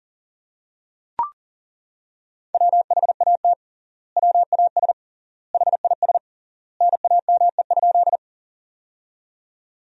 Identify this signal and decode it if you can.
Morse code, heard as a keyed tone, about 40 wpm, 700 Hz. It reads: WHAT WAS HIS NAME?
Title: 3x Word-Spacing - Sentences from Top 200 Words - 40wpm